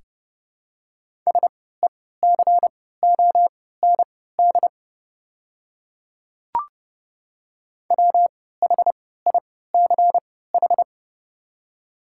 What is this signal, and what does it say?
Morse code, heard as a keyed tone, about 30 wpm, spaced as for 17 wpm, 700 Hz. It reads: SECOND WHICH
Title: 3x Character Spacing - Top 300 Words - 30wpm